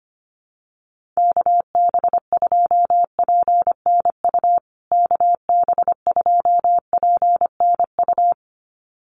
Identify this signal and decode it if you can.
Morse code, heard as a keyed tone, about 25 wpm, 700 Hz. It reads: KB2PNU KB2PNU
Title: USA Call Signs Repeated 2x 25wpm